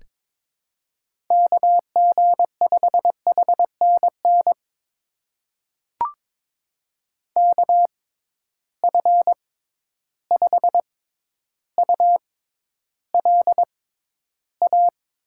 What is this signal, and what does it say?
Morse code, heard as a keyed tone, about 22 wpm, spaced as for 11 wpm, 700 Hz. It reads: KG5HNN KF5ULA
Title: USA Call Signs - 6x Character Spacing - 22wpm